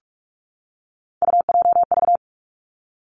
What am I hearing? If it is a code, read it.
Morse code, heard as a keyed tone, about 46 wpm, 700 Hz. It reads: UJV
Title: Three Challenging Letter-Number Combinations 45wpm